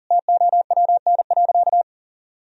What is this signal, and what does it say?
Morse code, heard as a keyed tone, about 40 wpm, 700 Hz. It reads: TOWN.
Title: Sentences from Top 1000 Words - Review Part 2 - 40wpm